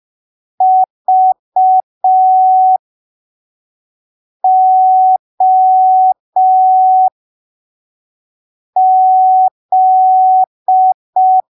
Morse code, heard as a keyed tone, about 5 wpm, spaced as for 3 wpm, 750 Hz. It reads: VOZ
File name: arquivos / texto / texto 04 5ppm